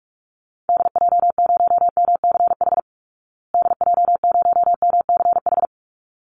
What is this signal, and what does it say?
Morse code, heard as a keyed tone, about 44 wpm, 700 Hz. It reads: DJ0MCH DJ0MCH
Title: Call Signs Repeated 3x - 45wpm